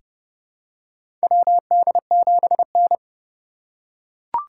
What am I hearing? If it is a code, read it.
Morse code, heard as a keyed tone, about 30 wpm, 700 Hz. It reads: WD7N
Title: USA Call Signs - 4 Characters - 2x Character Spacing - 30wpm